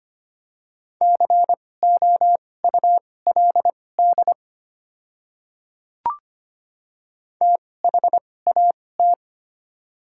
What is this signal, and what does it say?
Morse code, heard as a keyed tone, about 25 wpm, spaced as for 18 wpm, 690 Hz. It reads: COULD THAT
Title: 2x Character Spacing - Top 100 Words - 25wpm